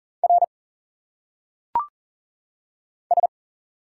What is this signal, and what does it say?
Morse code, heard as a keyed tone, about 40 wpm, 700 Hz. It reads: R S
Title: Single Letters - Phonetic Pronunciation 40wpm